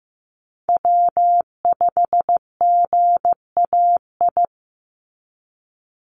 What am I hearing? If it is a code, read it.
Morse code, heard as a keyed tone, about 15 wpm, 700 Hz. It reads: W5GAI
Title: Call Signs Repeated 2x - 15wpm